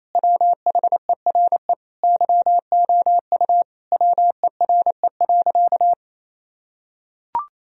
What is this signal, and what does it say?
Morse code, heard as a keyed tone, about 28 wpm, 700 Hz. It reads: WHERE YOU WERE.
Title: Sentences from Top 1000 Words - Review Part 4 - 28wpm